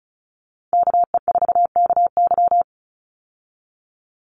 Morse code, heard as a keyed tone, about 34 wpm, 700 Hz. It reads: KE4KY
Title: Call Signs Repeated 2x - 35wpm